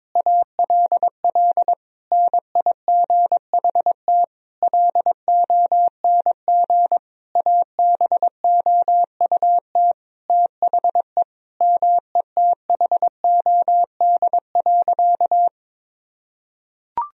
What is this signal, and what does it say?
Morse code, heard as a keyed tone, about 22 wpm, 700 Hz. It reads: ALL NIGHT LONG ABOUT THE METHOD.
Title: Sentences from Top 1000 Words - Review Part 3 - 22wpm